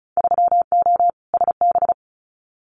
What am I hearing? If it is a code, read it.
Morse code, heard as a keyed tone, about 34 wpm, 700 Hz. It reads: 3O SB
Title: Winter Field Day Exchanges Encore 35wpm